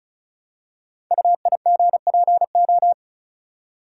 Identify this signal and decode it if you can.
Morse code, heard as a keyed tone, about 33 wpm, 700 Hz. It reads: UIGPO
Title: Code Groups - 5 Letters - 35wpm